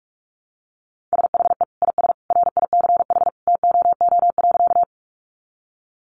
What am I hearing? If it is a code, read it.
Morse code, heard as a keyed tone, about 46 wpm, 700 Hz. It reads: SHE IS RICH TOO.
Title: Sentences from Top 1100 Words 45wpm